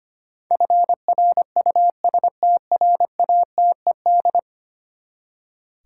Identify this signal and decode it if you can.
Morse code, heard as a keyed tone, about 25 wpm, 700 Hz. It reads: FRUSTRATED
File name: Ten Letter Words - 25wpm